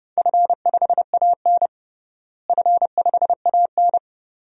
Morse code, heard as a keyed tone, about 30 wpm, 700 Hz. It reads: F5AN F5AN
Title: DXCC-100 Call Signs - 4 Characters - Repeated 2x - 30wpm